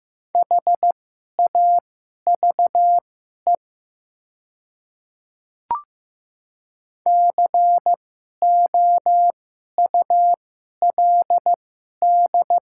Morse code, heard as a keyed tone, about 15 wpm, spaced as for 11 wpm, 700 Hz. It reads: HAVE COULD
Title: 2x Character Spacing - Top 100 Words - 15wpm